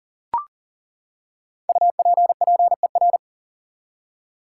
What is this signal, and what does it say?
Morse code, heard as a keyed tone, about 40 wpm, 700 Hz. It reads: UPPER
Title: Five Letter Words - 40wpm